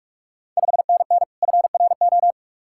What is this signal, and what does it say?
Morse code, heard as a keyed tone, about 46 wpm, 690 Hz. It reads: 5NN FRO
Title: Ontario QSO Party Exchange 45wpm